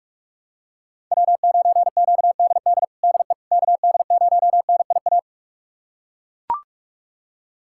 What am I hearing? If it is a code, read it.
Morse code, heard as a keyed tone, about 41 wpm, 700 Hz. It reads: W9QDD DE KD0NIA
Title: USA Callsign DE Callsign 45wpm